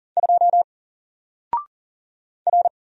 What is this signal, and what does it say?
Morse code, heard as a keyed tone, about 39 wpm, 700 Hz. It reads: 2 R
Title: Single Character (Letter:Number:Punctuation:Prosign) Alt 40wpm